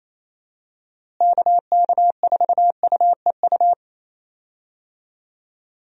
Morse code, heard as a keyed tone, about 27 wpm, 700 Hz. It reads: KK4UEU
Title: USA Call Signs - 6 Characters - Repeated 2x - 28wpm